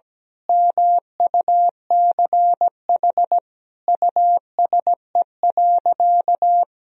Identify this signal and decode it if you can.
Morse code, heard as a keyed tone, about 17 wpm, 700 Hz. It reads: MUCH USE.
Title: Sentences from Top 1000 Words - Review Part 3 - 17wpm